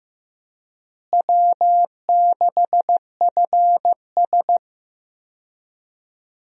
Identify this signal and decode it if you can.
Morse code, heard as a keyed tone, about 15 wpm, 700 Hz. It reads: W6FS
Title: USA Call Signs - 4 Characters - Repeated 2x - 15wpm